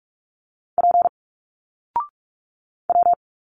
Morse code, heard as a keyed tone, about 46 wpm, 700 Hz. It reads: P W
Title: Single Letters 45wpm